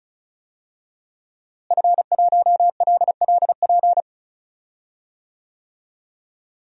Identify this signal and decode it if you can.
Morse code, heard as a keyed tone, about 36 wpm, 700 Hz. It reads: F1LLP
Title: DXCC-100 Call Signs - 5 Characters - Repeated 2x - 35wpm